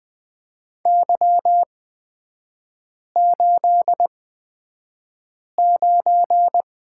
Morse code, heard as a keyed tone, about 20 wpm, spaced as for 5 wpm, 700 Hz. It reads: Y89
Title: 243 Lesson 39e - Letter z - Three Characters 20wpm